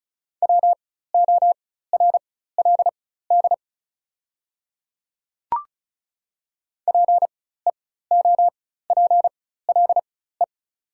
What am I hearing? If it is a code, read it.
Morse code, heard as a keyed tone, about 34 wpm, spaced as for 16 wpm, 690 Hz. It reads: WORLD PEOPLE
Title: 4x Character Spacing - Top 200 Words - 35wpm